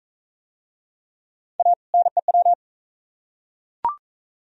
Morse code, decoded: A NEW

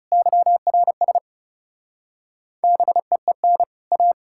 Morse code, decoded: YRS BEEN A

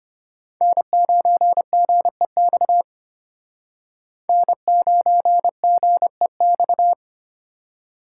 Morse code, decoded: N9GEX N9GEX

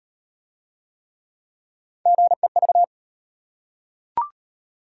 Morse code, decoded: GEV

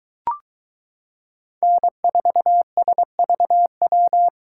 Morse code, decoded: N4SVW

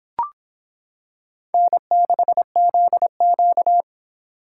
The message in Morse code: N6ZQ

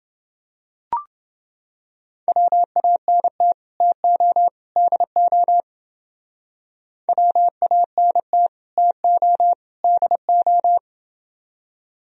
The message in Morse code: WANT TO DO WANT TO DO

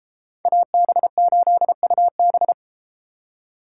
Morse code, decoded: AB8UB

